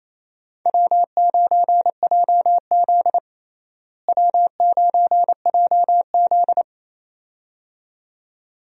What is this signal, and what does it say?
Morse code, heard as a keyed tone, about 28 wpm, 700 Hz. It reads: W9JZ W9JZ
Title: USA Call Signs - 4 Characters - Repeated 2x - 28wpm